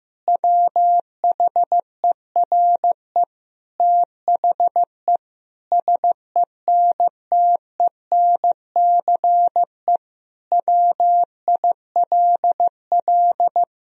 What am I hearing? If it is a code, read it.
Morse code, heard as a keyed tone, about 16 wpm, 700 Hz. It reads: WHERE THE SENTENCE WILL
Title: Sentences from Top 1000 Words - Review Part 1 - 15wpm